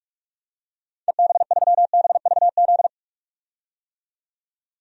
Morse code, heard as a keyed tone, about 46 wpm, 700 Hz. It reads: EB3BVZ
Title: DXCC-100 Call Signs - 6 Characters - 45wpm